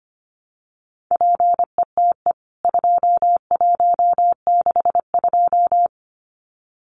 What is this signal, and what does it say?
Morse code, heard as a keyed tone, about 25 wpm, 700 Hz. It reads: PETE 2162
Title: CWT Exchange 25wpm